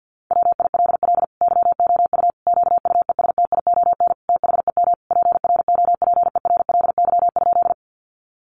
Morse code, heard as a keyed tone, about 50 wpm, 700 Hz. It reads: WILL YOU QUESTION THEM PROPERLY?
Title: Sentences from Top 1100 Words 50wpm